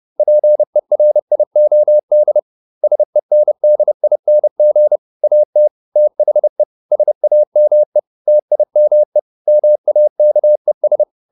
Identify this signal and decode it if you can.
Morse code, heard as a keyed tone, about 30 wpm, 600 Hz. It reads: PERIOD SENDING AT THE SAME TIME MAKES